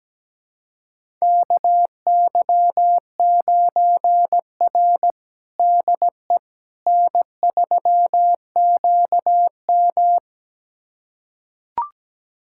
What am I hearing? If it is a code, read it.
Morse code, heard as a keyed tone, about 17 wpm, 700 Hz. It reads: KY9R DE N3QM
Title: USA Callsign DE Callsign - 4 Characters - 17wpm